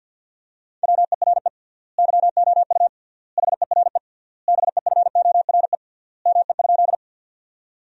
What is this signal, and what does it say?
Morse code, heard as a keyed tone, about 52 wpm, 700 Hz. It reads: WERE YOU HERE BEFORE ME?
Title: 3x Word-Spacing - Sentences from Top 200 Words - 50wpm